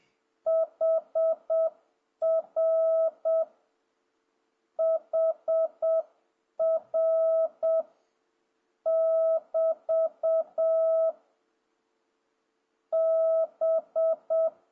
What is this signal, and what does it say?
Morse code, heard as a keyed tone, about 7 wpm, 650 Hz. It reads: HR HR <BT> B
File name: jh8cbh / 2026cw